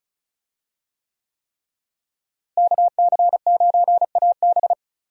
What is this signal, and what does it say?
Morse code, heard as a keyed tone, about 36 wpm, 700 Hz. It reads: KC9AB
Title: USA Call Signs - 5 Characters - Repeated 3x - 35wpm